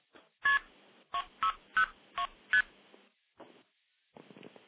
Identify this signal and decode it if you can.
Multi-frequency trunk signalling, heard as fast pairs of tones, tones of about 90 ms. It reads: KP 602ST